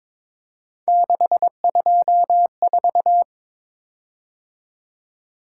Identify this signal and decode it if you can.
Morse code, heard as a keyed tone, about 22 wpm, 700 Hz. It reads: 624
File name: practice / 22wpm / Three Letter-Number Combinations - 22wpm